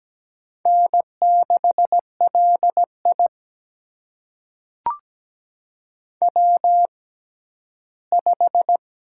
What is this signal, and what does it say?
Morse code, heard as a keyed tone, about 17 wpm, spaced as for 11 wpm, 700 Hz. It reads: N6LI W5
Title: USA Call Signs - 4 Characters - 6x Character Spacing - 17wpm